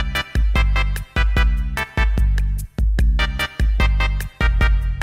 There is no code.